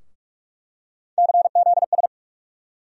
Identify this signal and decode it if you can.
Morse code, heard as a keyed tone, about 45 wpm, 700 Hz. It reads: CZS
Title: USA Call Sign Postfixes - 3 Letters 45wpm